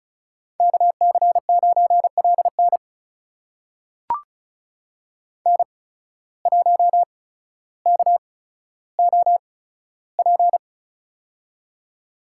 Morse code, heard as a keyed tone, about 36 wpm, spaced as for 15 wpm, 700 Hz. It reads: KC9LN N1KOP
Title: USA Call Signs - 8x Character Spacing - 35wpm